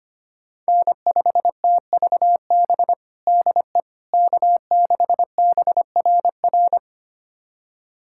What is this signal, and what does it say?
Morse code, heard as a keyed tone, about 25 wpm, 700 Hz. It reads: N5TVB DE K6BRR
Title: USA Callsign DE Callsign - 5 Characters - 25wpm